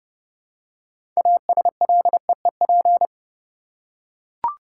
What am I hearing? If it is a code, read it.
Morse code, heard as a keyed tone, about 31 wpm, 700 Hz. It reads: ASLEEP
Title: Six Letter Words - 30wpm